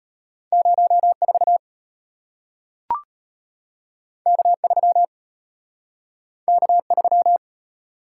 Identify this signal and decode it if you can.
Morse code, heard as a keyed tone, about 34 wpm, 700 Hz. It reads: 04 K3 K3